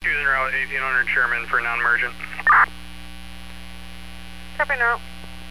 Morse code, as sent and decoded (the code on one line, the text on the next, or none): none